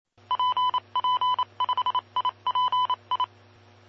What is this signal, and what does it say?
Morse code, heard as a keyed tone, about 27 wpm, 1000 Hz. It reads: PP5IPI
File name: aulas de telegrafia